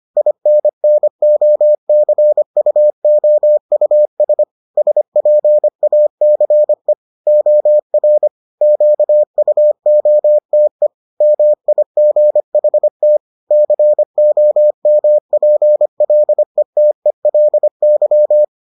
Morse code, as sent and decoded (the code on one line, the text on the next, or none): .. -. -. --- -.-. ..- --- ..- ... / ... .--. .- -.-. / --- .-. / --.- ..- --- - . / -- .. --. .... - / -.-. --- -- .--. .-.. - . .-.. -.--
INNOCUOUS SPACE OR QUOTE MIGHT COMPLETELY